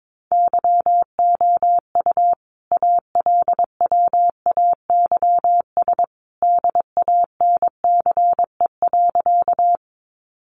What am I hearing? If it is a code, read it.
Morse code, heard as a keyed tone, about 22 wpm, 700 Hz. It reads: YOU ALWAYS DANCE.
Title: Sentences from Top 700 Words 22wpm